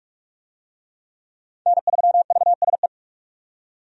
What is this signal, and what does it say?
Morse code, heard as a keyed tone, about 44 wpm, 700 Hz. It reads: N3VSE
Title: USA Call Signs - 5 Characters - 45wpm